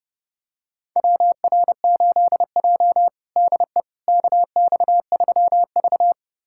Morse code, heard as a keyed tone, about 30 wpm, 700 Hz. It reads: WR8J DE KX3V
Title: USA Callsign DE Callsign - 4 Characters - 30wpm